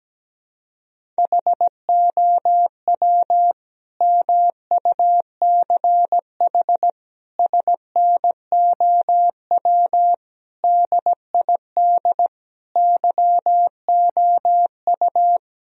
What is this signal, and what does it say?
Morse code, heard as a keyed tone, about 17 wpm, 700 Hz. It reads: HOW MUCH SNOW DID YOU
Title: Sentences from Top 1000 Words - Review Part 3 - 17wpm